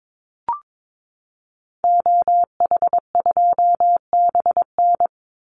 Morse code, heard as a keyed tone, about 22 wpm, 700 Hz. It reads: OH2BN
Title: Call Signs Repeated 2x - 22wpm